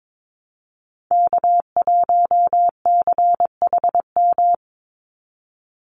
Morse code, heard as a keyed tone, about 22 wpm, 700 Hz. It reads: K1CHM
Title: Call Signs Repeated 3x - 22wpm